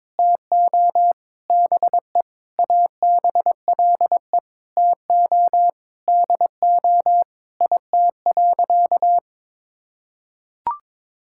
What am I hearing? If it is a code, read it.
Morse code, heard as a keyed tone, about 22 wpm, 700 Hz. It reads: TO BE ABLE TO DO IT.
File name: practice / 22wpm / Sentences with 4-Word Phrases - 22wpm